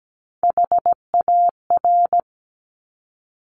Morse code, heard as a keyed tone, about 17 wpm, 700 Hz. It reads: HAR
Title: Top Three Letter Combinations in Words 17wpm